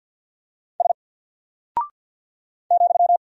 Morse code, heard as a keyed tone, about 48 wpm, 700 Hz. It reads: S ,
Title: Single Character (Letter:Number:Punctuation:Prosign) Alt 50wpm